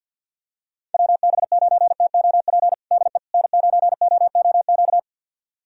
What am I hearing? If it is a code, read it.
Morse code, heard as a keyed tone, about 46 wpm, 690 Hz. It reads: WB9TOP DE N9OOQ
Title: USA Callsign DE Callsign 50wpm